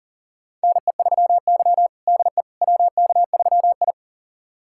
Morse code, heard as a keyed tone, about 40 wpm, 700 Hz. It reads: NE3Y DE WK3I
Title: USA Callsign DE Callsign - 4 Characters - 40wpm